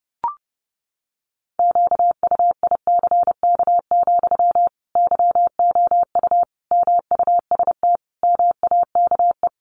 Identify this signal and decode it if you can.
Morse code, heard as a keyed tone, about 30 wpm, 700 Hz. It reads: QUICK, YOU MUST MAKE
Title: Sentences from Top 500 Words 30wpm